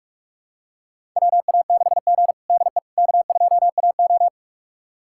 Morse code, heard as a keyed tone, about 41 wpm, 700 Hz. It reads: WA6G DE K2AO
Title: USA Callsign DE Callsign - 4 Characters - 45wpm